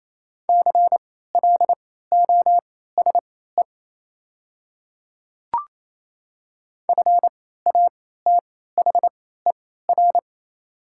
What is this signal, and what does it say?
Morse code, heard as a keyed tone, about 28 wpm, spaced as for 16 wpm, 690 Hz. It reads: CLOSE FATHER
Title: 3x Character Spacing - Top 500 Words - 28wpm